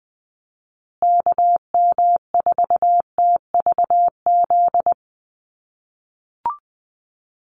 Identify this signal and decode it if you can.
Morse code, heard as a keyed tone, about 20 wpm, 700 Hz. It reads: KM4TVZ
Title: USA Call Signs Repeated 3x 20wpm